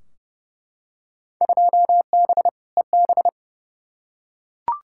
Morse code, heard as a keyed tone, about 32 wpm, 700 Hz. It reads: 2B EB
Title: Field Day Exchanges - 30wpm